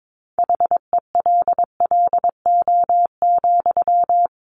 Morse code, heard as a keyed tone, about 22 wpm, 700 Hz. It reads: HELLO,
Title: Sentences from Top 1100 Words 22wpm